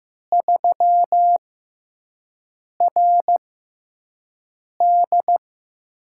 Morse code, heard as a keyed tone, about 15 wpm, spaced as for 5 wpm, 700 Hz. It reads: 3RD
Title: USA Call Signs - 6x Character Spacing - 15wpm